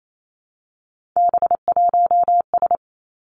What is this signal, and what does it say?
Morse code, heard as a keyed tone, about 28 wpm, 700 Hz. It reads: B1S
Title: Three Challenging Letter-Number Combinations 28wpm